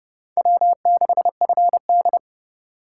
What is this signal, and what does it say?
Morse code, heard as a keyed tone, about 32 wpm, 700 Hz. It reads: W6FD